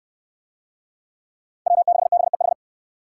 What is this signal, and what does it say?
Morse code, heard as a keyed tone, about 60 wpm, 700 Hz. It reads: W6BEH